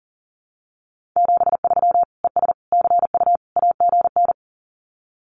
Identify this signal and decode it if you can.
Morse code, heard as a keyed tone, about 40 wpm, 700 Hz. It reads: 73 ES CU AGN
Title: Common QSO Phrases Encore 40wpm